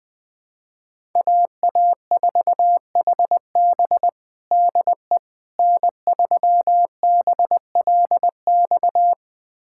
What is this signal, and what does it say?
Morse code, heard as a keyed tone, about 21 wpm, 700 Hz. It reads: AA4HB DE N3BLX
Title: USA Callsign DE Callsign - 5 Characters - 20wpm